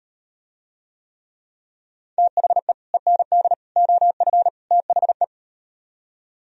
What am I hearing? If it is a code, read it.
Morse code, heard as a keyed tone, about 38 wpm, 700 Hz. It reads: THE END OF THE